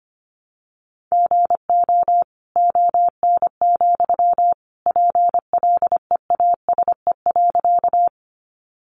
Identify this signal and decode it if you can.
Morse code, heard as a keyed tone, about 25 wpm, 700 Hz. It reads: GO ON, PLEASE.
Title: Sentences from Top 1500 Words 25wpm